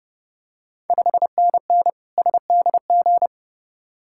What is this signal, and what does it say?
Morse code, heard as a keyed tone, about 30 wpm, 700 Hz. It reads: 5NN SDG
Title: Ontario QSO Party Exchange 30wpm